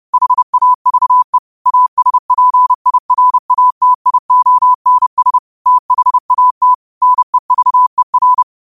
Morse code, heard as a keyed tone, about 30 wpm, 1000 Hz. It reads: HAVE ASPIRATIONS THAT NEVER
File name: qotdfiles / 30wpm